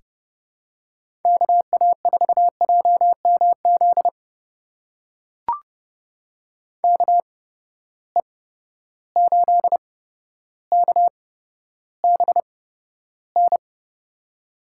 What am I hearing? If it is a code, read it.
Morse code, heard as a keyed tone, about 30 wpm, spaced as for 13 wpm, 700 Hz. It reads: KA4JMZ KE8KBN
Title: USA Call Signs - 6 Characters - 8x Character Spacing - 30wpm